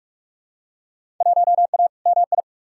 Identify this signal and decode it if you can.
Morse code, heard as a keyed tone, about 46 wpm, 700 Hz. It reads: 1A MI
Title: Field Day Exchanges - 45wpm